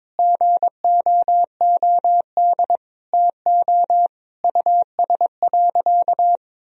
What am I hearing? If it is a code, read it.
Morse code, heard as a keyed tone, about 22 wpm, 700 Hz. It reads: GOOD TO US.
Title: Sentences from Top 1000 Words - Review Part 3 - 22wpm